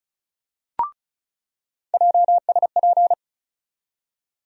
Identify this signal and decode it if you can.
Morse code, heard as a keyed tone, about 36 wpm, 700 Hz. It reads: JSP
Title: USA Call Sign Postfixes - 3 Letters 35wpm